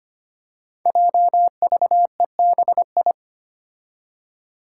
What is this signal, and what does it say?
Morse code, heard as a keyed tone, about 25 wpm, 700 Hz. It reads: JVEBI
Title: Five Challenging Letter-Number Combinations 25wpm